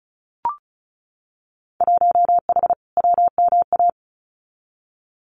Morse code, heard as a keyed tone, about 34 wpm, 700 Hz. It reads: 1H WMA
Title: Winter Field Day Exchanges 35wpm